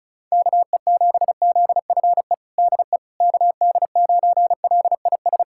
Morse code, heard as a keyed tone, about 34 wpm, 700 Hz. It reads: KE7ZFE DE KD9LIS
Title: USA Callsign DE Callsign 35wpm